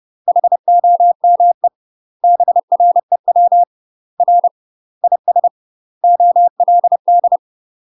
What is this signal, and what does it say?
Morse code, heard as a keyed tone, about 30 wpm, 700 Hz. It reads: HOME BREW R IS OLD
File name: UK MASTER QSOs TRAINING / 30WPM / UK MASTER QSOs 0035